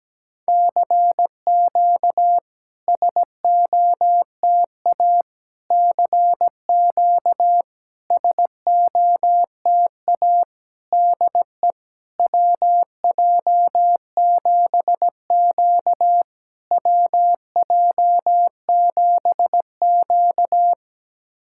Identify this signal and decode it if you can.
Morse code, heard as a keyed tone, about 17 wpm, 700 Hz. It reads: CQ SOTA CQ SOTA DE WJ7Q WJ7Q
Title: SOTA QSO Practice 17wpm